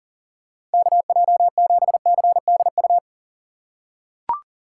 Morse code, heard as a keyed tone, about 40 wpm, 700 Hz. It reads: KJ7CDU
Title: USA Call Signs - 4x Character Spacing - 40wpm